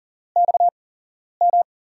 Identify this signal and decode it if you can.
Morse code, heard as a keyed tone, about 40 wpm, spaced as for 11 wpm, 700 Hz. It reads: XM
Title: USA Call Signs - 5 Characters - 8x Character Spacing - 40wpm